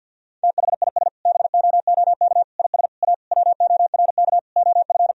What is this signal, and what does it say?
Morse code, heard as a keyed tone, about 52 wpm, 700 Hz. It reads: THIS BOOK IS A WORK OF
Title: Sentences from Top 1000 Words - Review Part 3 - 50wpm